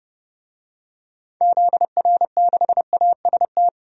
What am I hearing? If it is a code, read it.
Morse code, heard as a keyed tone, about 32 wpm, 700 Hz. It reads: ZR6AST
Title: DXCC-100 Call Signs - 6 Characters - 30wpm